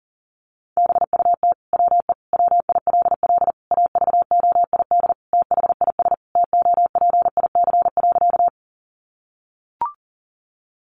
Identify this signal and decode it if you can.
Morse code, heard as a keyed tone, about 40 wpm, 700 Hz. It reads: BUT WE WILL AVOID THIS TOPIC.
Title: Sentences from Top 1200 Words 40wpm